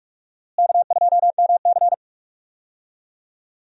Morse code, heard as a keyed tone, about 46 wpm, 700 Hz. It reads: K2MC
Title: USA Call Signs - 4 Characters - Repeated 2x - 45wpm